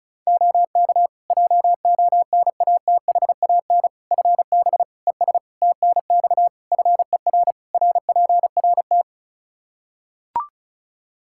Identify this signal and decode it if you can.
Morse code, heard as a keyed tone, about 36 wpm, 700 Hz. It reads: OK JONATHAN FB ES TNX FER RPRT
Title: Practice QSO - Exchange 2 - Info - 35wpm